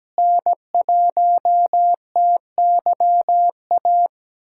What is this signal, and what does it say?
Morse code, heard as a keyed tone, about 17 wpm, 700 Hz. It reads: N1TYA
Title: USA Callsign DE Callsign - 5 Characters - 17wpm